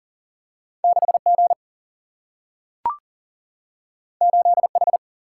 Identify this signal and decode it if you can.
Morse code, heard as a keyed tone, about 39 wpm, 700 Hz. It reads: 6 G 8 H
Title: Single-Letter Single-Number 40wpm